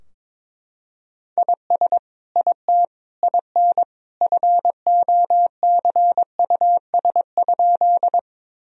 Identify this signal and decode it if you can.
Morse code, heard as a keyed tone, about 22 wpm, 700 Hz. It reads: IS IT IN FOCUS?
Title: Sentences from Top 1000 Words - Review Part 4 - 22wpm